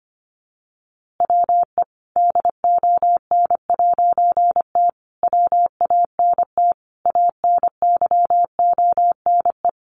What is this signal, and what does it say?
Morse code, heard as a keyed tone, about 25 wpm, 700 Hz. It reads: WE DON'T WANT ANYONE
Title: A Sound Of Thunder 25wpm